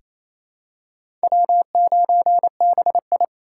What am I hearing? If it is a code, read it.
Morse code, heard as a keyed tone, about 28 wpm, 700 Hz. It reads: W9BI
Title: USA Call Signs - 4 Characters - 2x Character Spacing - 28wpm